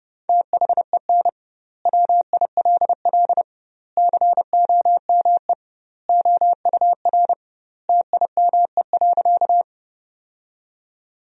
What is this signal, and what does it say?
Morse code, heard as a keyed tone, about 32 wpm, 700 Hz. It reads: THEN WILL COME OUR TIME.
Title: 2x Word-Spacing - Sentences from Top 200 Words - 30wpm